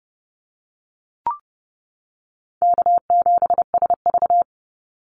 Morse code, heard as a keyed tone, about 30 wpm, 700 Hz. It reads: K7SV
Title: Call Signs - 30wpm